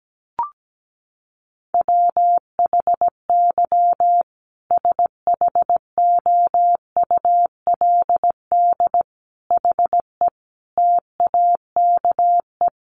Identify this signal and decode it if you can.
Morse code, heard as a keyed tone, about 17 wpm, 700 Hz. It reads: WHY SHOULD HE TAKE
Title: Sentences from Top 1500 Words 17wpm